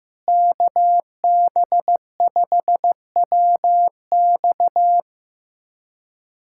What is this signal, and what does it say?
Morse code, heard as a keyed tone, about 15 wpm, 700 Hz. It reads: KB5WX
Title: USA Callsign DE Callsign - 5 Characters - 15wpm